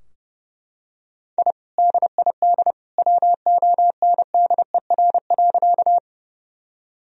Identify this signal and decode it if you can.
Morse code, heard as a keyed tone, about 30 wpm, 700 Hz. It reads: I DID WONDER.
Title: Sentences from Top 1000 Words - Review Part 3 - 30wpm